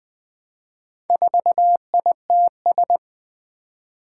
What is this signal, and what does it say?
Morse code, heard as a keyed tone, about 20 wpm, 700 Hz. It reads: 4ITS
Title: Four Challenging Letter-Number Combinations 20wpm